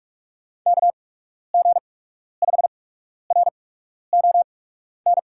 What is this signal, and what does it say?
Morse code, heard as a keyed tone, about 46 wpm, spaced as for 12 wpm, 700 Hz. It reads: KG5RON
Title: USA Call Signs - 6 Characters - 8x Character Spacing - 45wpm